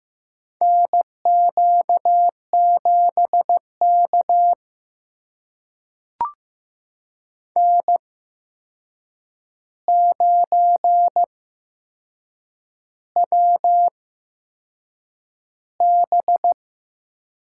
Morse code, heard as a keyed tone, about 15 wpm, spaced as for 6 wpm, 700 Hz. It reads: NQ7K N9WB
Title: USA Call Signs - 4 Characters - 8x Character Spacing - 15wpm